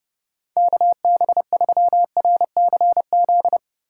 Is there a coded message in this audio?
Yes: Morse code from 0.6 to 3.6 s, heard as a keyed tone, about 29 wpm, 700 Hz. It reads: KB3RCZ